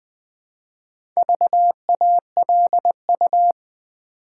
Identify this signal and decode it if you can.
Morse code, heard as a keyed tone, about 20 wpm, 700 Hz. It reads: VALU
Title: Build A Word - 2 Letters At A Time - 8-Letter Words - 20wpm